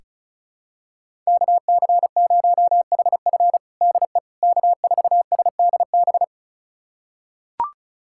Morse code, heard as a keyed tone, about 34 wpm, 700 Hz. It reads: KC0HF DE K4SDB
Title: USA Callsign DE Callsign - 5 Characters - 35wpm